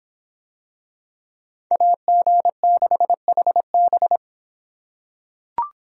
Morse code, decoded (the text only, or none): AG6HB